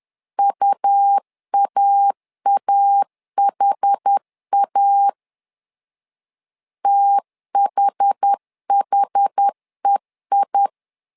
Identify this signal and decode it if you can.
Morse code, heard as a keyed tone, about 10 wpm, 800 Hz. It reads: UAAHA THHEI
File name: LEZIONI / 2 UHDN / 2 UH 10wpm 1714